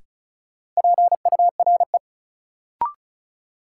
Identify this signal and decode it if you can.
Morse code, heard as a keyed tone, about 36 wpm, 700 Hz. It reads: PURE
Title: Sentences from Top 1500 Words - New Word Drill - 35wpm